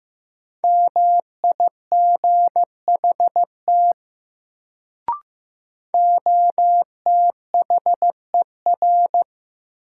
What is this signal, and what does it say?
Morse code, heard as a keyed tone, about 15 wpm, 700 Hz. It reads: MIGHT OTHER
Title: Sets of 2 Words - New Word Drill - 15wpm